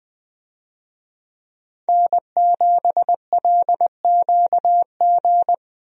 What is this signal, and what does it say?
Morse code, heard as a keyed tone, about 20 wpm, 700 Hz. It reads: N7LQG
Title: USA Call Signs - 5 Characters - Repeated 3x - 20wpm